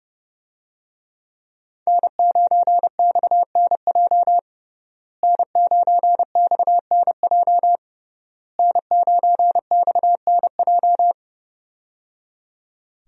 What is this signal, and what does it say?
Morse code, heard as a keyed tone, about 30 wpm, 700 Hz. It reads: N9XNJ N9XNJ N9XNJ
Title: USA Call Signs - 5 Characters - Repeated 3x - 30wpm